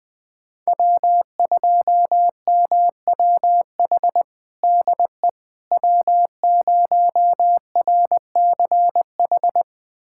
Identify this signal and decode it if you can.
Morse code, heard as a keyed tone, about 20 wpm, 700 Hz. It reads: W2MWH DE W0RCH
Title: USA Callsign DE Callsign - 5 Characters - 20wpm